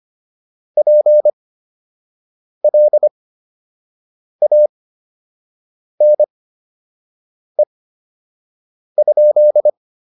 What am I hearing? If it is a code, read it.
Morse code, heard as a keyed tone, about 25 wpm, spaced as for 6 wpm, 600 Hz. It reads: PLANE?